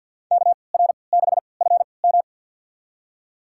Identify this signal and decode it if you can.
Morse code, heard as a keyed tone, about 46 wpm, spaced as for 27 wpm, 700 Hz. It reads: KR6FM
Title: USA Call Signs - 5 Characters - 3x Character Spacing - 50wpm